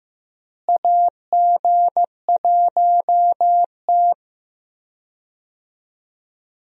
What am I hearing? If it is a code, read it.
Morse code, heard as a keyed tone, about 15 wpm, 700 Hz. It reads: AG1T
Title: USA Call Signs - 4 Characters - Repeated 3x - 15wpm